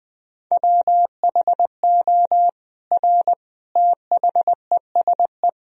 Morse code, heard as a keyed tone, about 20 wpm, 700 Hz. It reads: WHO R THESE
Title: Sentences from Top 200 Words and Abbreviations 20wpm